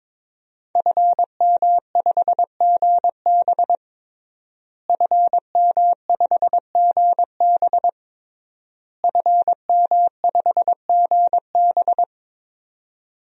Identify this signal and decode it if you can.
Morse code, heard as a keyed tone, about 22 wpm, 700 Hz. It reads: FM5GB FM5GB FM5GB
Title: DXCC-100 Call Signs - 5 Characters - Repeated 3x - 22wpm